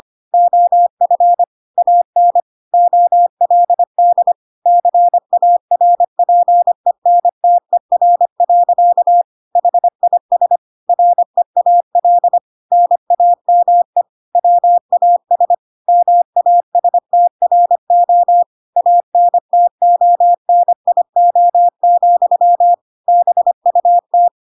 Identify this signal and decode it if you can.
Morse code, heard as a keyed tone, about 25 wpm, 700 Hz. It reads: OF AN OLD CARPENTER. HIS REAL NAME WAS MASTRO ANTONIO, BUT
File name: Pinocchio / Pinocchio ch1 25 25 ik1hge